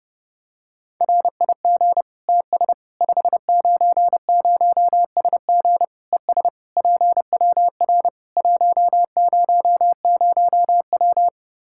Morse code, heard as a keyed tone, about 32 wpm, 700 Hz. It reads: RIG TS 590SG ES PWR 100W
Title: K1USN SST QSO Practice 30wpm